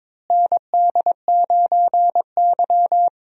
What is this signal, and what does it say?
Morse code, heard as a keyed tone, about 23 wpm, 700 Hz. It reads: ND9Y
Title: USA Call Signs - 4 Characters - Repeated 3x - 22wpm